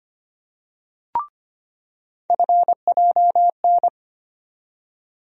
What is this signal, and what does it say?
Morse code, heard as a keyed tone, about 25 wpm, 700 Hz. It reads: FJN
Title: USA Call Sign Postfixes - 3 Letters 25wpm